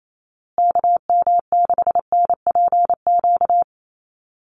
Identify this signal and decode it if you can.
Morse code, heard as a keyed tone, about 28 wpm, 700 Hz. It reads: KM6NPQ
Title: USA Call Signs Repeated 3x 28wpm